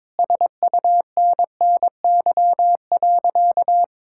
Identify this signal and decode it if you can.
Morse code, heard as a keyed tone, about 21 wpm, 700 Hz. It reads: SUNNY.
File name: practice / 22wpm / Sentences with 4-Word Phrases - 22wpm